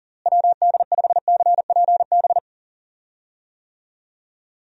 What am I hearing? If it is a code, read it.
Morse code, heard as a keyed tone, about 40 wpm, 700 Hz. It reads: WD5CPB
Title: USA Call Signs - 6 Characters - Repeated 2x - 40wpm